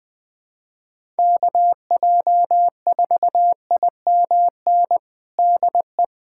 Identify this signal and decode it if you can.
Morse code, heard as a keyed tone, about 20 wpm, 700 Hz. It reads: KJ4IMN DE
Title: USA Callsign DE Callsign - 6 Characters - 20wpm